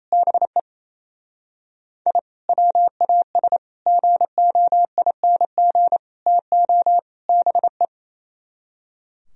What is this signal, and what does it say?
Morse code, heard as a keyed tone, about 36 wpm, 700 Hz. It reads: BE I WAS GOING TO BE